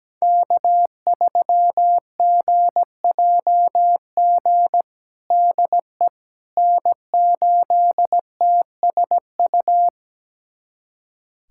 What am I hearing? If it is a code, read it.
Morse code, heard as a keyed tone, about 17 wpm, 700 Hz. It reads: K3GJG DE N8TSU
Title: USA Callsign DE Callsign - 5 Characters - 17wpm